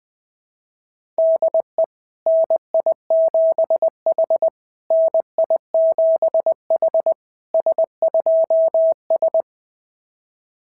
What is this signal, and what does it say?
Morse code, heard as a keyed tone, about 19 wpm, 650 Hz. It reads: DE NI7H NI7H S2S
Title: SOTA QSO Practice 20wpm